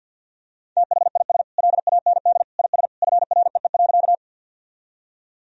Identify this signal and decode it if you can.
Morse code, heard as a keyed tone, about 51 wpm, 700 Hz. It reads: THIS LAND IS FREE.
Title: Sentences from Top 1000 Words - Review Part 1 - 50wpm